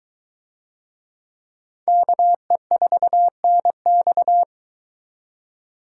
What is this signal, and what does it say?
Morse code, heard as a keyed tone, about 21 wpm, 700 Hz. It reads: KE4NX